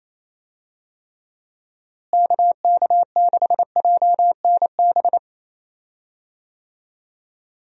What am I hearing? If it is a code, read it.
Morse code, heard as a keyed tone, about 28 wpm, 700 Hz. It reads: KK6JNB
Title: USA Call Signs - 6 Characters - Repeated 3x - 28wpm